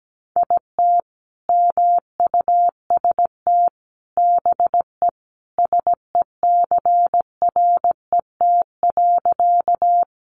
Morse code, heard as a keyed tone, about 17 wpm, 700 Hz. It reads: IT MUST BE SECRET.
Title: Sentences from Top 1200 Words 17wpm